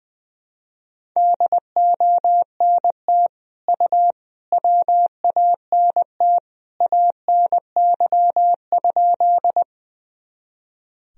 Morse code, heard as a keyed tone, about 21 wpm, 700 Hz. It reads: DONT U WANT ANY?
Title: Sentences from Top 300 Words and Abbreviations 20wpm